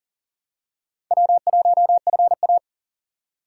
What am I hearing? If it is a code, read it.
Morse code, heard as a keyed tone, about 40 wpm, 700 Hz. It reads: W1FA
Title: USA Call Signs - 2x Character Spacing - 40wpm